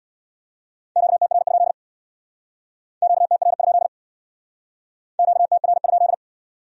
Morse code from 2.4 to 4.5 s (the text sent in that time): QTR?